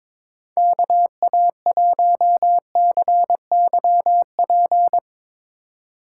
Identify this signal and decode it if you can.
Morse code, heard as a keyed tone, about 22 wpm, 700 Hz. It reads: KA1CYP
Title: USA Call Signs - 6 Characters - 4x Character Spacing - 22wpm